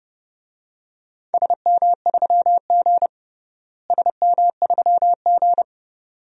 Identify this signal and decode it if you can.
Morse code, heard as a keyed tone, about 32 wpm, 700 Hz. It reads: SM3G SM3G
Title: DXCC-100 Call Signs - 4 Characters - Repeated 2x - 30wpm